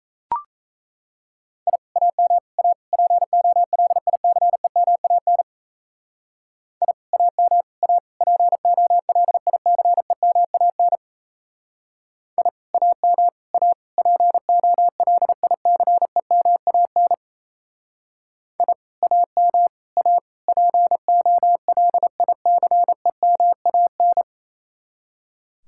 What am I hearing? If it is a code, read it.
Morse code, heard as a keyed tone, about 35 wpm, 700 Hz. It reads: I AM A POLICEMAN I AM A POLICEMAN I AM A POLICEMAN I AM A POLICEMAN
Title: Speed-Racing Common QSO Phrases - 28wpm